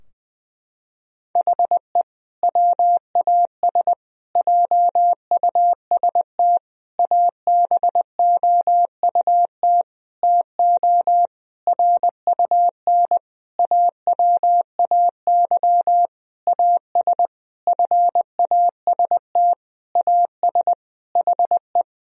Morse code, heard as a keyed tone, about 21 wpm, 700 Hz. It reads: HE WAS JUST ABOUT TO RUN AWAY AS FAST AS HE
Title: Sentences from Top 1000 Words - Review Part 2 - 20wpm